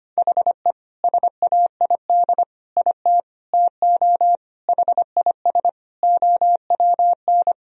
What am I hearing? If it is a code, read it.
Morse code, heard as a keyed tone, about 25 wpm, 700 Hz. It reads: HE SAID IT TO HIS OWN
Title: Sentences from Top 1000 Words - Review Part 1 - 25wpm